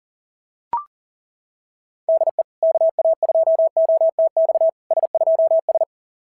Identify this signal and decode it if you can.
Morse code, heard as a keyed tone, about 40 wpm, 690 Hz. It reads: DE KA2OTX S2S
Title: SOTA QSO Practice 40wpm